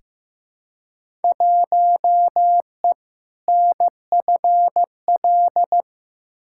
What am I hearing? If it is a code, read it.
Morse code, heard as a keyed tone, about 15 wpm, 700 Hz. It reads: 1E NFL
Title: Field Day Exchanges - 15wpm